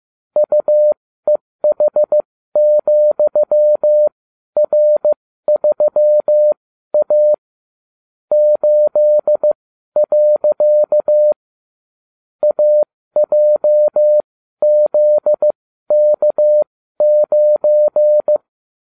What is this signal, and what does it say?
Morse code, heard as a keyed tone, about 15 wpm, spaced as for 12 wpm, 600 Hz. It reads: UEH,R3A 8. AJZK9
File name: cw / ubacw30 3